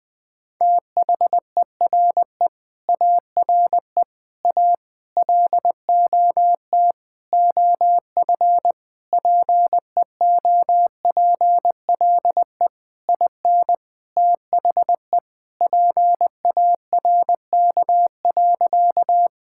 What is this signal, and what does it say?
Morse code, heard as a keyed tone, about 20 wpm, 700 Hz. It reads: THERE ARE A LOT OF PEOPLE IN THE PARK.
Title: Sentences with 5-Word Phrases - 20wpm